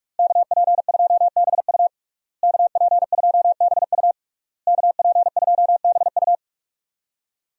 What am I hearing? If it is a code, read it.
Morse code, heard as a keyed tone, about 46 wpm, 700 Hz. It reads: KP2BU KP2BU KP2BU
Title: USA Call Signs - 5 Characters - Repeated 3x - 45wpm